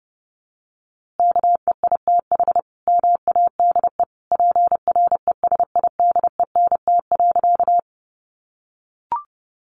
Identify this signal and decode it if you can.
Morse code, heard as a keyed tone, about 30 wpm, 700 Hz. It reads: KEITH MADE PRESIDENT.
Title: A Sound Of Thunder 30wpm